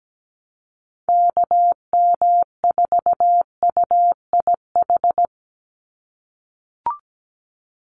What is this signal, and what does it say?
Morse code, heard as a keyed tone, about 17 wpm, 700 Hz. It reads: KM4UIH
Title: USA Call Signs 17wpm